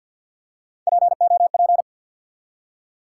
Morse code, heard as a keyed tone, about 48 wpm, 700 Hz. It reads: POP
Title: Three Letter Words - 50wpm